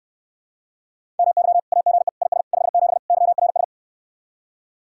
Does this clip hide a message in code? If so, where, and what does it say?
Yes: Morse code from 1.2 to 3.6 s, heard as a keyed tone, about 69 wpm, 700 Hz. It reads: MY AGE IS 57 YRS